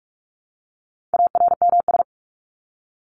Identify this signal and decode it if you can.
Morse code, heard as a keyed tone, about 44 wpm, 700 Hz. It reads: ARMS
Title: ES:S Suffix - Top 100 Words with Base Words 45wpm